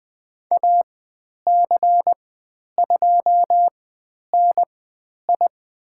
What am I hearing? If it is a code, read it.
Morse code, heard as a keyed tone, about 20 wpm, spaced as for 10 wpm, 700 Hz. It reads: AC2NI
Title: 102 Lesson 17g - Letter u - Call Signs 20wpm